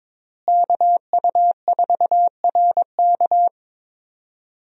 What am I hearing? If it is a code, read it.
Morse code, heard as a keyed tone, about 22 wpm, 700 Hz. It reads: KU4RK
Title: USA Call Signs - 5 Characters - Repeated 2x - 22wpm